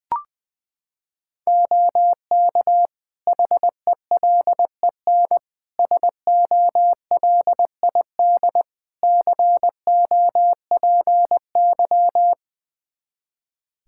Morse code, heard as a keyed tone, about 20 wpm, 700 Hz. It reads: OK HELEN SOLID COPY
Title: Practice QSO - Exchange 3 - More Info - 20wpm